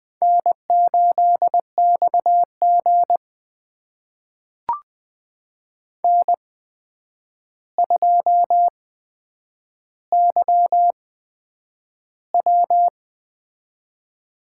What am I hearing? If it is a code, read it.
Morse code, heard as a keyed tone, about 20 wpm, spaced as for 9 wpm, 700 Hz. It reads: N8XG N2YW
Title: USA Call Signs - 4 Characters - 8x Character Spacing - 20wpm